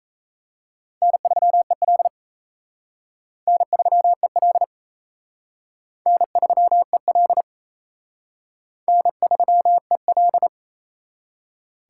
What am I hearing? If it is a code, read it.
Morse code, heard as a keyed tone, about 37 wpm, 700 Hz. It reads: N3EL N3EL N3EL N3EL